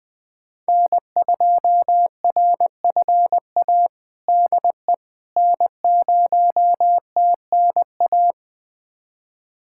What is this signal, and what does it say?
Morse code, heard as a keyed tone, about 21 wpm, 700 Hz. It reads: N2RFA DE N0TNA
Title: USA Callsign DE Callsign - 5 Characters - 20wpm